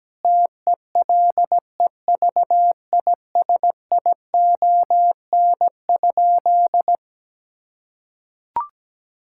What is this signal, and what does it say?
Morse code, heard as a keyed tone, about 17 wpm, 700 Hz. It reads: TELEVISION?
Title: Sentences from Top 1000 Words - Review Part 5 - 17wpm